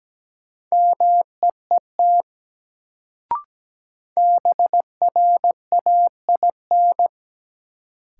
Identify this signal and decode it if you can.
Morse code, heard as a keyed tone, about 17 wpm, 700 Hz. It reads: MEET BRAIN
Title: Sentences from Top 1000 Words - Review Part 6 - New Word Drill - 17wpm